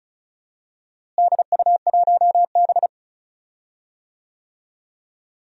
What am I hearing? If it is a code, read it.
Morse code, heard as a keyed tone, about 36 wpm, 700 Hz. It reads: DU1B
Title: DXCC-100 Call Signs - 4 Characters - Repeated 3x - 35wpm